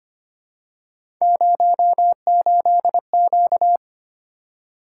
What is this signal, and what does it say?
Morse code, heard as a keyed tone, about 25 wpm, 700 Hz. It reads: 08Q
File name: Three Letter-Number Combinations - 25wpm